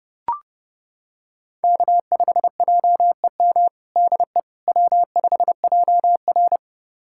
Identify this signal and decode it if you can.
Morse code, heard as a keyed tone, about 30 wpm, 700 Hz. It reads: K5JEM DE W5JR
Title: USA Callsign DE Callsign 30wpm